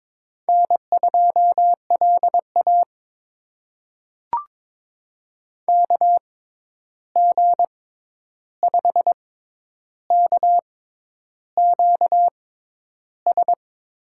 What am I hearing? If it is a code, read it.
Morse code, heard as a keyed tone, about 22 wpm, spaced as for 10 wpm, 700 Hz. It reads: N2LA KG5KQS
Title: USA Call Signs - 6x Character Spacing - 22wpm